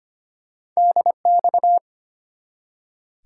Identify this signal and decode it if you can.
Morse code, heard as a keyed tone, about 25 wpm, 700 Hz. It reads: DX
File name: practice / 25wpm / ARRL Sections v2 - 25wpm